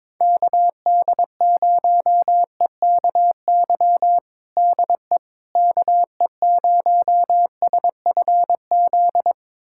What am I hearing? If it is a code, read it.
Morse code, heard as a keyed tone, about 22 wpm, 700 Hz. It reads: KD0EKY DE KE0SFZ
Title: USA Callsign DE Callsign - 6 Characters - 22wpm